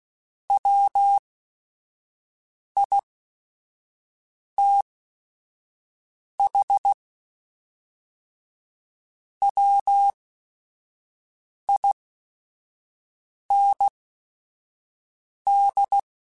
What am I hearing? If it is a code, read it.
Morse code, heard as a keyed tone, about 16 wpm, spaced as for 5 wpm, 800 Hz. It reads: WITH WIND
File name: Weather Sets / Weather Sets-mp3s / Weather 4 10 @5